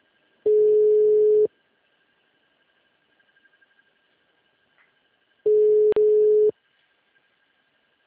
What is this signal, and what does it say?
Ringback tone (Europe, 425 Hz).
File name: out- -2028- - - .6996